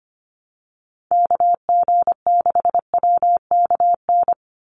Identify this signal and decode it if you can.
Morse code, heard as a keyed tone, about 25 wpm, 700 Hz. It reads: KG6WKN
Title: USA Call Signs 25wpm